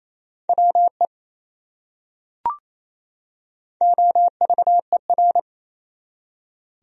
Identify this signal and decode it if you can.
Morse code, heard as a keyed tone, about 28 wpm, 700 Hz. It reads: WE OVER
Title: Top 100 Words 28wpm